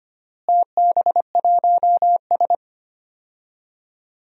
Morse code, heard as a keyed tone, about 25 wpm, 700 Hz. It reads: TB1S
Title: DXCC-100 Call Signs - 4 Characters - Repeated 3x - 25wpm